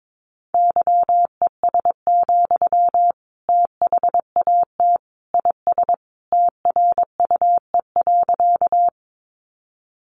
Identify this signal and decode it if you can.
Morse code, heard as a keyed tone, about 22 wpm, 700 Hz. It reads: YES, THAT IS TRUE.
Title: Sentences from Top 500 Words 22wpm